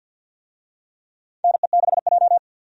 Morse code, heard as a keyed tone, about 53 wpm, 700 Hz. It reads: NE6J
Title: USA Call Signs - 4 Characters - 50wpm